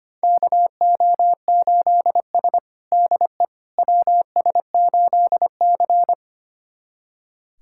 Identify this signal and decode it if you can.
Morse code, heard as a keyed tone, about 25 wpm, 700 Hz. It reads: KO8S DE WS8C
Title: USA Callsign DE Callsign - 4 Characters - 25wpm